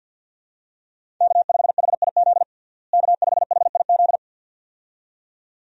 Morse code, heard as a keyed tone, about 47 wpm, 700 Hz. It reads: K5HIZ K5HIZ
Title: USA Call Signs - 5 Characters - Repeated 2x - 50wpm